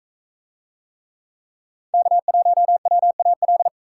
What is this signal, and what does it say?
Morse code, heard as a keyed tone, about 41 wpm, 700 Hz. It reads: K1WAL